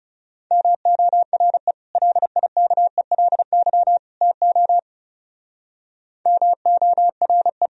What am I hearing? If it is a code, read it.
Morse code, heard as a keyed tone, about 36 wpm, 700 Hz. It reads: MORE LIKELY TO MORE
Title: Speed-Racing - Sets of 3 Words 30wpm